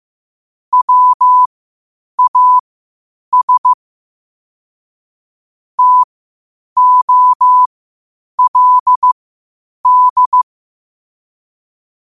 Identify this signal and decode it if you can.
Morse code, heard as a keyed tone, about 15 wpm, spaced as for 8 wpm, 1000 Hz. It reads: WAS TOLD
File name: qotdfiles / 8wpm